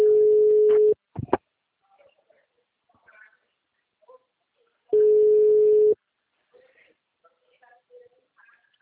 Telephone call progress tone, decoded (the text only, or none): ringback tone